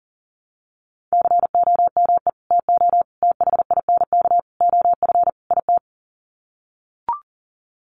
Morse code, heard as a keyed tone, about 40 wpm, 700 Hz. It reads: COME TO THINK OF IT
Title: Sets of 5 Words Encore 40wpm